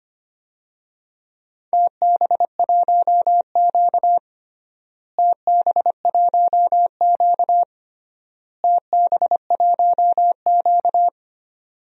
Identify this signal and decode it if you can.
Morse code, heard as a keyed tone, about 25 wpm, 700 Hz. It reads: TB1Q TB1Q TB1Q
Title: DXCC-100 Call Signs - 4 Characters - Repeated 3x - 25wpm